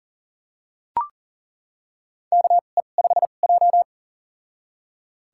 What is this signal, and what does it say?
Morse code, heard as a keyed tone, about 40 wpm, spaced as for 29 wpm, 700 Hz. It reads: KE5J
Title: USA Call Signs - 4 Characters - 2x Character Spacing - 40wpm